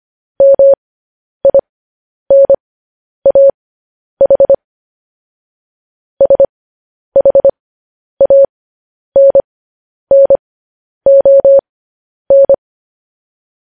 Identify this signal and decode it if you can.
Morse code, heard as a keyed tone, about 25 wpm, spaced as for 10 wpm, 550 Hz. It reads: MINAH SHANNON
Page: news-headlines-in-morse-code-at-10-wpm